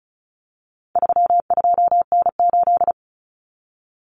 Morse code, heard as a keyed tone, about 34 wpm, 700 Hz. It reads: 32N8
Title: Common Cut Numbers - 4-Digit 35wpm